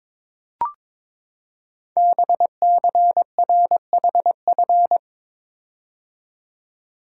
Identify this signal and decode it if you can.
Morse code, heard as a keyed tone, about 22 wpm, 700 Hz. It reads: BCRHF
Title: Code Groups - 5 Letters - 22wpm